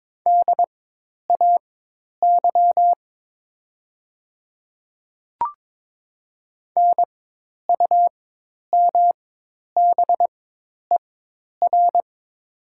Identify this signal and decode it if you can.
Morse code, heard as a keyed tone, about 22 wpm, spaced as for 10 wpm, 700 Hz. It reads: DAY NUMBER
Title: 4x Character Spacing - Top 100 Words - 22wpm